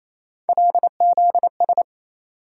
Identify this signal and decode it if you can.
Morse code, heard as a keyed tone, about 28 wpm, 700 Hz. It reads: LZS
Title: USA Call Sign Postfixes - 3 Letters 28wpm